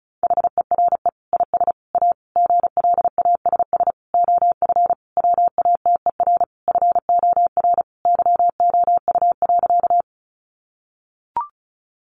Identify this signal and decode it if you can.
Morse code, heard as a keyed tone, about 34 wpm, 700 Hz. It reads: HERE IS A GLASS OF WATER FOR YOU.
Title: Sentences from Top 1300 Words 35wpm